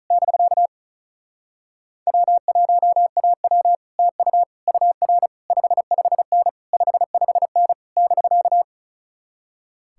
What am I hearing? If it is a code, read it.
Morse code, heard as a keyed tone, about 46 wpm, 700 Hz. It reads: <BK> W1AW TU UR 55N 55N <BK>